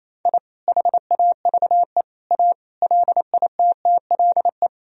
Morse code, heard as a keyed tone, about 28 wpm, 700 Hz. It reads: I HAVE A LITTLE
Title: Sentences from Top 1000 Words - Review Part 1 - 28wpm